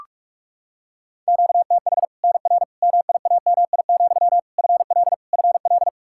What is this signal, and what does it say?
Morse code, heard as a keyed tone, about 41 wpm, 700 Hz. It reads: QTH NR MIAMI, FL FL
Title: Practice QSO - Exchange 1 - Beginning - 45wpm